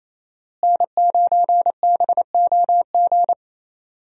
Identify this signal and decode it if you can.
Morse code, heard as a keyed tone, about 28 wpm, 700 Hz. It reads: N9BOG
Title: USA Call Signs - 5 Characters - Repeated 3x - 28wpm